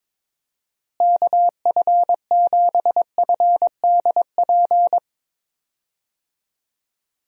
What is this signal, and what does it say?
Morse code, heard as a keyed tone, about 22 wpm, 700 Hz. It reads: KF7FDP
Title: USA Call Signs - 6 Characters - Repeated 3x - 22wpm